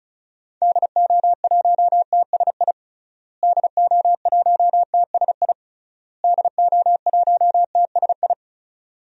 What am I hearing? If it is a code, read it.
Morse code, heard as a keyed tone, about 35 wpm, 700 Hz. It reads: DO1TSI DO1TSI DO1TSI
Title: DXCC-100 Call Signs - 6 Characters - Repeated 3x - 35wpm